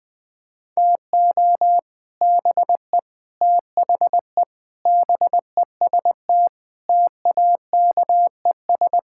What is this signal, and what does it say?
Morse code, heard as a keyed tone, about 20 wpm, 700 Hz. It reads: TO BE THE BEST TAKES
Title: Sentences with 3-Word Phrases - 20wpm